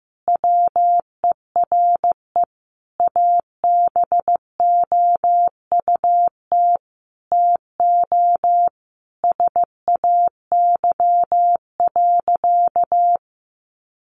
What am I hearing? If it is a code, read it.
Morse code, heard as a keyed tone, about 15 wpm, 700 Hz. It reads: WERE ABOUT TO SAY.
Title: Sentences from Top 1200 Words 15wpm